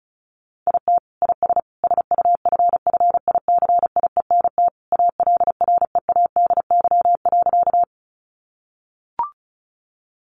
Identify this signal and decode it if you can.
Morse code, heard as a keyed tone, about 34 wpm, 700 Hz. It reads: IT IS SUFFICIENT ALREADY.
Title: Sentences from Top 1400 Words 35wpm